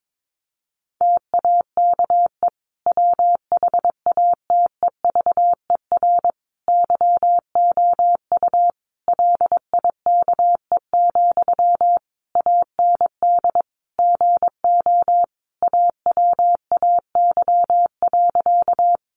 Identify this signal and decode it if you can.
Morse code, heard as a keyed tone, about 22 wpm, 700 Hz. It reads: TAKE WHATEVER YOU LIKE, AND GO AWAY.
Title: Sentences from Top 1100 Words 22wpm